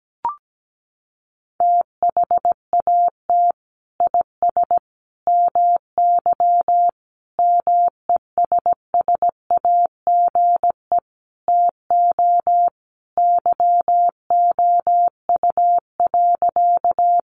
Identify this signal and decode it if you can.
Morse code, heard as a keyed tone, about 17 wpm, 700 Hz. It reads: THAT IS MY MESSAGE TO YOU.
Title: Sentences from Top 1100 Words 17wpm